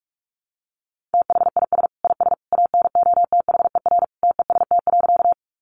Morse code, heard as a keyed tone, about 44 wpm, 700 Hz. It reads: THIS IS ANOTHER TEST.
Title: Sentences from Top 600 Words 45wpm